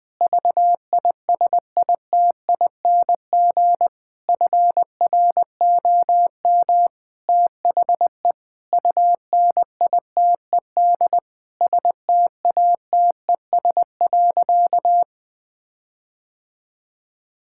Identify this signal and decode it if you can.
Morse code, heard as a keyed tone, about 20 wpm, 700 Hz. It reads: VISITING FROM THE UNITED STATES.
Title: Sentences with 4-Word Phrases - 20wpm